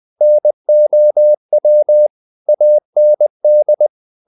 Morse code, heard as a keyed tone, about 20 wpm, 600 Hz. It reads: NOW AND
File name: the old man and the sea part 2 20wpm